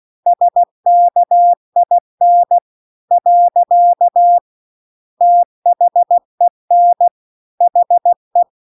Morse code, heard as a keyed tone, about 16 wpm, 700 Hz. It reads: SKIN . THEN HE
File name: CW-The-Big-Twoheaded-River-2-16WPM